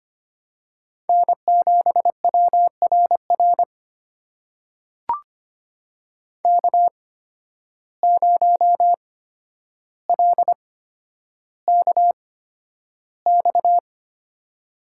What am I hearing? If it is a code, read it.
Morse code, heard as a keyed tone, about 25 wpm, spaced as for 11 wpm, 700 Hz. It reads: N7WRR K0LKX
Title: USA Call Signs - 5 Characters - 8x Character Spacing - 25wpm